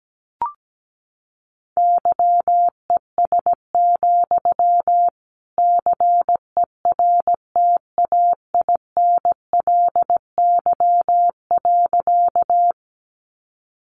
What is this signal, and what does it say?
Morse code, heard as a keyed tone, about 17 wpm, 700 Hz. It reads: YES, CERTAINLY.
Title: Sentences from Top 900 Words 17wpm